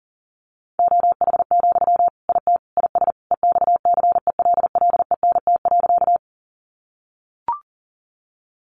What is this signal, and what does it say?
Morse code, heard as a keyed tone, about 40 wpm, 700 Hz. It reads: OH, IT IS EXCELLENT.
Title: Sentences from Top 1000 Words 40wpm